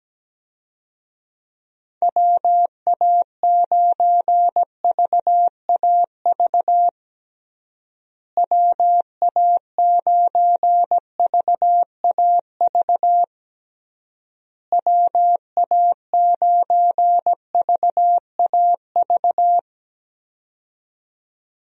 Morse code, heard as a keyed tone, about 17 wpm, 700 Hz. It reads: WA9VAV WA9VAV WA9VAV
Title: USA Call Signs - 6 Characters - Repeated 3x - 17wpm